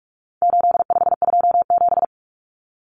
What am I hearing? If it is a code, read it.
Morse code, heard as a keyed tone, about 44 wpm, 700 Hz. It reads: 8527
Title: Common Cut Numbers - 4-Digit 45wpm